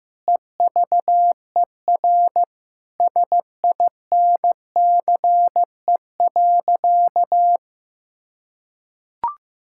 Morse code, heard as a keyed tone, about 15 wpm, 700 Hz. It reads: EVER SINCE.
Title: Sentences from Top 1000 Words - Review Part 5 - 15wpm